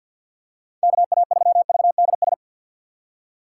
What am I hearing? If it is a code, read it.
Morse code, heard as a keyed tone, about 53 wpm, 700 Hz. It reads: KA3VDS
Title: USA Call Signs - 6 Characters - 4x Character Spacing - 50wpm